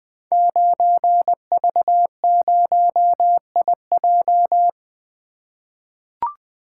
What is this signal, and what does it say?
Morse code, heard as a keyed tone, about 20 wpm, 700 Hz. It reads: 9V0IJ
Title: Code Groups - 5 Letter:Numbers - 20wpm